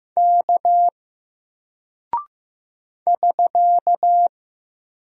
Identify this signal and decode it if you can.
Morse code, heard as a keyed tone, about 15 wpm, 700 Hz. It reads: K <SK>
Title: Single Character (Letter:Number:Punctuation:Prosign) Alt 15wpm